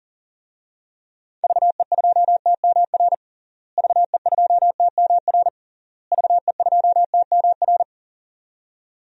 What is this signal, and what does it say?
Morse code, heard as a keyed tone, about 40 wpm, 700 Hz. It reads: VE2TMR VE2TMR VE2TMR
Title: DXCC-100 Call Signs - 6 Characters - Repeated 3x - 40wpm